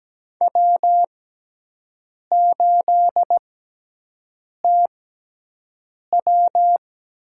Morse code, heard as a keyed tone, about 17 wpm, spaced as for 6 wpm, 700 Hz. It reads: W8TW
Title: USA Call Signs - 5 Characters - 6x Character Spacing - 17wpm